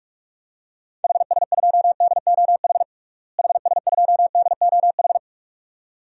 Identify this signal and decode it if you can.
Morse code, heard as a keyed tone, about 42 wpm, 700 Hz. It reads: HS2DOH HS2DOH
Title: DXCC-100 Call Signs - 6 Characters - Repeated 2x - 45wpm